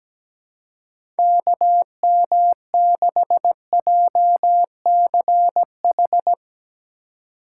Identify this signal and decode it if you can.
Morse code, heard as a keyed tone, about 17 wpm, 700 Hz. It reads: KM6JCH
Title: USA Call Signs - 2x Character Spacing - 17wpm